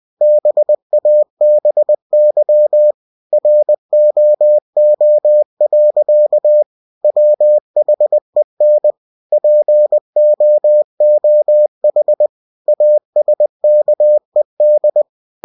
Morse code, heard as a keyed tone, about 20 wpm, 600 Hz. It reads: BABY ROO. WHEN POOH ASKED